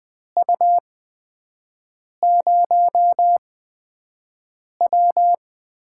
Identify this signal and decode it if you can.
Morse code, heard as a keyed tone, about 20 wpm, spaced as for 5 wpm, 700 Hz. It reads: U0W
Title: USA Call Signs - 4 Characters - 8x Character Spacing - 20wpm